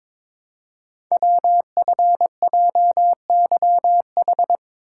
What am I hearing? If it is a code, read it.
Morse code, heard as a keyed tone, about 22 wpm, 700 Hz. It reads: WFJYH
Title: Code Groups - 5 Letters - 22wpm